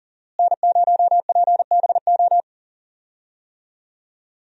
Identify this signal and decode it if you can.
Morse code, heard as a keyed tone, about 40 wpm, 700 Hz. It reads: N0PBO
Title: USA Call Signs - 5 Characters - Repeated 2x - 40wpm